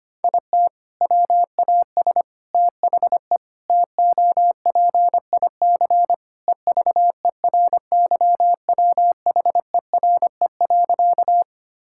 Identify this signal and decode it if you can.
Morse code, heard as a keyed tone, about 25 wpm, 700 Hz. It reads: IT WAS THE TOPIC EVERYWHERE.
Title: Sentences from Top 1600 Words 25wpm